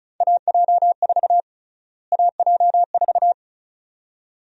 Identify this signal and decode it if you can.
Morse code, heard as a keyed tone, about 34 wpm, 700 Hz. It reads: AJ4 AJ4
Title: USA Call Sign Prefixes - Repeated 2x - 35wpm